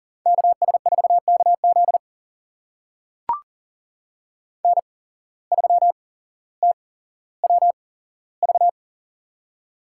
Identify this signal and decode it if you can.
Morse code, heard as a keyed tone, about 40 wpm, spaced as for 17 wpm, 700 Hz. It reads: KS4KZ N3TWV
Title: USA Call Signs - 5 Characters - 8x Character Spacing - 40wpm